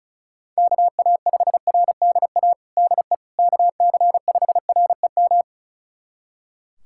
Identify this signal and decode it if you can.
Morse code, heard as a keyed tone, about 36 wpm, 700 Hz. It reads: KA5RDA DE KC5REM
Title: USA Callsign DE Callsign 35wpm